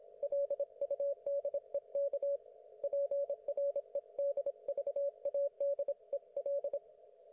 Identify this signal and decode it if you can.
Morse code, heard as a keyed tone, about 26 wpm, 560 Hz. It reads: LUDEK PREDVADEL
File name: Archiv / CUC194 / ok1fhi cul194